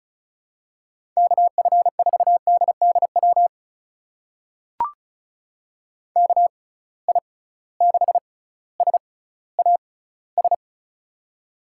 Morse code, heard as a keyed tone, about 36 wpm, spaced as for 18 wpm, 690 Hz. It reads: KF4DDW KI6SAS
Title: USA Call Signs - 6 Characters - 6x Character Spacing - 35wpm